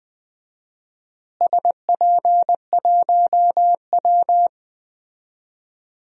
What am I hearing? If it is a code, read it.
Morse code, heard as a keyed tone, about 20 wpm, 700 Hz. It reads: SP1W
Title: DXCC-100 Call Signs - 4 Characters - Repeated 3x - 20wpm